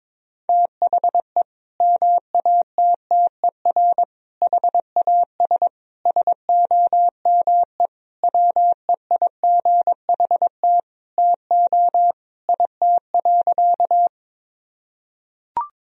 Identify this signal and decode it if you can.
Morse code, heard as a keyed tone, about 22 wpm, 700 Hz. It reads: THE MATTER HAS SOME WEIGHT TO IT.
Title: Sentences from Top 1000 Words - Review Part 3 - 22wpm